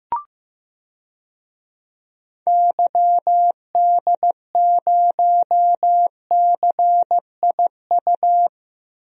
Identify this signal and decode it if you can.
Morse code, heard as a keyed tone, about 15 wpm, 700 Hz. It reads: YD0CIU
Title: DXCC-100 Call Signs - 6 Characters - 15wpm